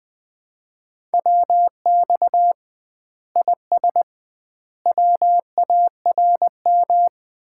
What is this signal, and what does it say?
Morse code, heard as a keyed tone, about 20 wpm, 700 Hz. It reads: WX IS WARM
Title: Common QSO Phrases - 2x Word-Spacing - 20wpm